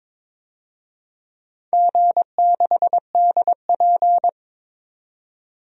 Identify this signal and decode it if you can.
Morse code, heard as a keyed tone, about 22 wpm, 700 Hz. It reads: G6DP
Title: DXCC-100 Call Signs - 4 Characters - 22wpm